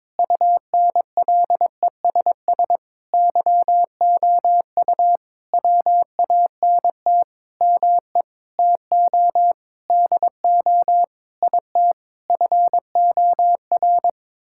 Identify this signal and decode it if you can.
Morse code, heard as a keyed tone, about 22 wpm, 700 Hz. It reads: UNLESS YOU WANT ME TO DO IT FOR
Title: Sentences from Top 1000 Words - Review Part 6 - 22wpm